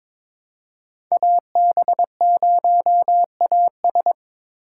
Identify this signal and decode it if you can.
Morse code, heard as a keyed tone, about 21 wpm, 700 Hz. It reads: AB0AS
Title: USA Call Signs - 5 Characters - 4x Character Spacing - 22wpm